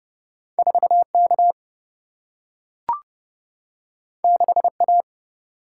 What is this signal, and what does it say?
Morse code, heard as a keyed tone, about 30 wpm, 700 Hz. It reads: 4K 6A
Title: Single-Letter Single-Number 30wpm